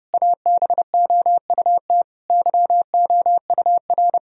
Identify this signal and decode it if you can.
Morse code, heard as a keyed tone, about 28 wpm, 700 Hz. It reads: ABOUT YOUR
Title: Sentences from Top 1000 Words - Review Part 4 - 30wpm